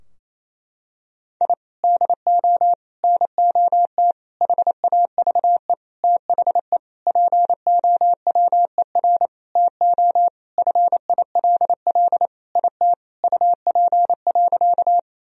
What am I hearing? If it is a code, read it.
Morse code, heard as a keyed tone, about 28 wpm, 700 Hz. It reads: I DO NOT HAVE THE POWER TO FILL IT UP.
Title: Sentences from Top 1000 Words - Review Part 3 - 28wpm